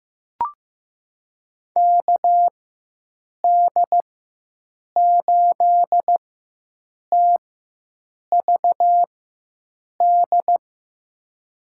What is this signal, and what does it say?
Morse code, heard as a keyed tone, about 15 wpm, spaced as for 7 wpm, 700 Hz. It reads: KD8TVD
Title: USA Call Signs - 4x Character Spacing - 15wpm